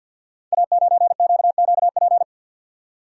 Morse code, heard as a keyed tone, about 53 wpm, 700 Hz. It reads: A9QQP
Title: Code Groups - 5 Letter:Numbers - 50wpm